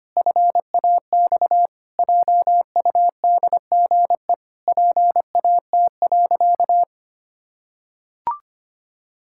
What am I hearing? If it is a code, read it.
Morse code, heard as a keyed tone, about 25 wpm, 700 Hz. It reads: FAX JUDGE PAT.
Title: Pangram Challenge v2 - 25wpm